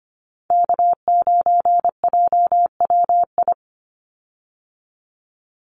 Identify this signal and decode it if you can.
Morse code, heard as a keyed tone, about 25 wpm, 700 Hz. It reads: K9JWI
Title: Call Signs Repeated 2x - 25wpm